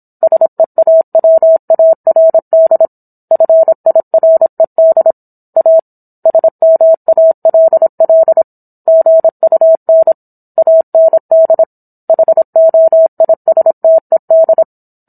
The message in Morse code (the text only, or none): SEAWARD FIRED A SMALL GUN AND HOISTED